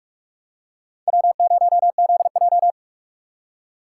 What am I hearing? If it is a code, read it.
Morse code, heard as a keyed tone, about 42 wpm, 700 Hz. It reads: W0ZJ
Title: USA Call Signs - 4 Characters - 2x Character Spacing - 45wpm